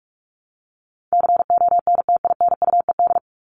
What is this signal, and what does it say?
Morse code, heard as a keyed tone, about 44 wpm, 700 Hz. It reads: CONTINUED
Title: ED Suffix - Top 100 Words with Base Words 45wpm